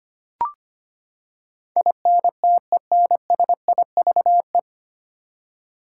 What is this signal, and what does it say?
Morse code, heard as a keyed tone, about 25 wpm, 700 Hz. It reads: INTENSIVE
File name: Nine Letter Words - 25wpm